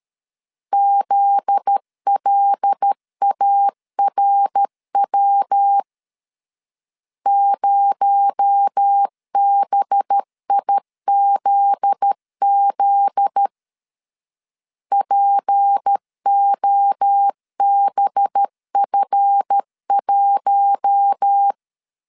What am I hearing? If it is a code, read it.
Morse code, heard as a keyed tone, about 13 wpm, 800 Hz. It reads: ZLARW 0BIZZ POBF1
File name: LEZIONI / 7 GF9Z / 7 GF9Z 12wpm 2147